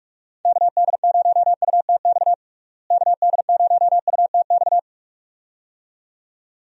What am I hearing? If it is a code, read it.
Morse code, heard as a keyed tone, about 46 wpm, 700 Hz. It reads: KD0UTX KD0UTX
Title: USA Call Signs - 6 Characters - Repeated 2x - 45wpm